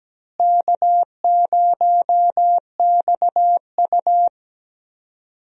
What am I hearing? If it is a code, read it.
Morse code, heard as a keyed tone, about 17 wpm, 700 Hz. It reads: K0XU
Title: USA Call Signs - 4 Characters - Repeated 2x - 17wpm